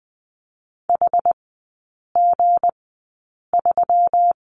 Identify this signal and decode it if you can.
Morse code, heard as a keyed tone, about 20 wpm, spaced as for 8 wpm, 690 Hz. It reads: HG3
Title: 159 Lesson 26e - Letter g - Three Characters 20wpm